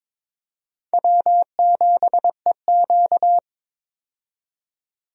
Morse code, heard as a keyed tone, about 22 wpm, 700 Hz. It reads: W7EQ
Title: USA Call Signs - 4 Characters - 22wpm